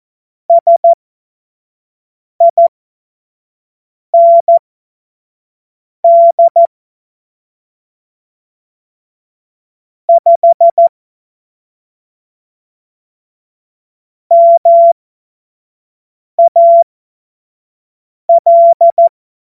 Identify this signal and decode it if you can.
Morse code, heard as a keyed tone, about 14 wpm, spaced as for 5 wpm, 680 Hz. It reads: SIND 5 MAL
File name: Morsekurs / Witze / Witz 06. Juli 2021, 25 F70